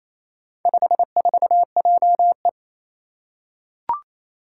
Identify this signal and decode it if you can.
Morse code, heard as a keyed tone, about 27 wpm, 700 Hz. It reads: 54JE
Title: Four Challenging Letter-Number Combinations 28wpm